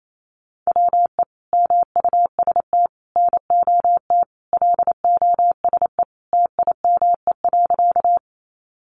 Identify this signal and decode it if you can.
Morse code, heard as a keyed tone, about 28 wpm, 700 Hz. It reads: WE MUST NOT LOSE TIME.
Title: Sentences from Top 1100 Words 28wpm